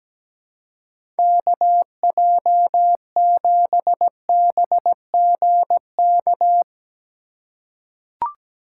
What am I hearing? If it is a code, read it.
Morse code, heard as a keyed tone, about 17 wpm, 700 Hz. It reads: KJ7BGK
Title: USA Call Signs - 6 Characters - 4x Character Spacing - 17wpm